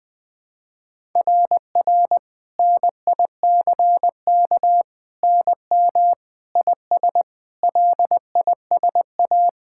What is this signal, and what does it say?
Morse code, heard as a keyed tone, about 20 wpm, 700 Hz. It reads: RR NICK NM IS LISA
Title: SOTA QSO Practice 20wpm